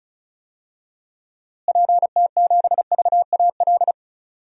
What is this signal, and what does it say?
Morse code, heard as a keyed tone, about 36 wpm, 700 Hz. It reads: PT7VAL
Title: DXCC-100 Call Signs - 6 Characters - Repeated 3x - 35wpm